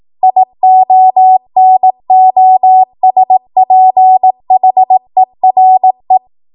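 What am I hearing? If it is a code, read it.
Morse code, heard as a keyed tone, about 18 wpm, 750 Hz. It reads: IONOSPHERE
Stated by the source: ARRL 18 wpm Morse code practice archive transmission.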